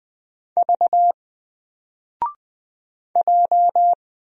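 Morse code, heard as a keyed tone, about 20 wpm, 700 Hz. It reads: V J
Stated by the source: Single letter and number Morse drill at 20 wpm, from N4LCD.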